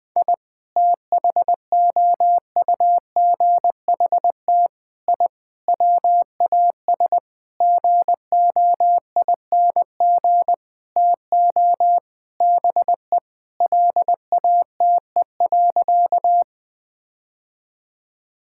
Morse code, heard as a keyed tone, about 21 wpm, 700 Hz. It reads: I THOUGHT I WAS GOING TO BE LATE.
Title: Sentences with 6-Word Phrases - 20wpm